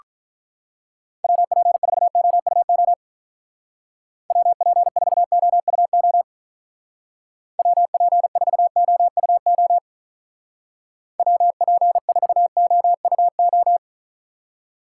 Morse code, heard as a keyed tone, about 44 wpm, 700 Hz. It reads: WP4OUO WP4OUO WP4OUO WP4OUO